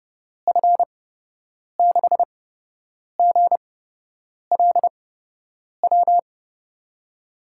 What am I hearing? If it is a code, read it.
Morse code, heard as a keyed tone, about 32 wpm, spaced as for 8 wpm, 700 Hz. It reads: F6GLW
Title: USA Call Signs - 6 Characters - 8x Character Spacing - 30wpm